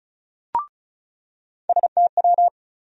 Morse code, heard as a keyed tone, about 36 wpm, 700 Hz. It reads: STW
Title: USA Call Sign Postfixes - 3 Letters 35wpm